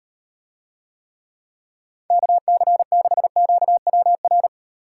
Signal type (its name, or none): Morse code